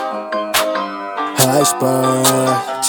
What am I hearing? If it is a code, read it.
There is no code.